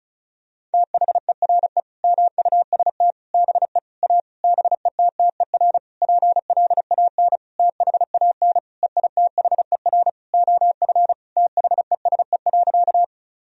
Morse code, heard as a keyed tone, about 36 wpm, 700 Hz. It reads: THERE MUST BE A BETTER PLAN THAN EITHER OF THESE.
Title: Sentences from Top 1000 Words - Review Part 4 - 35wpm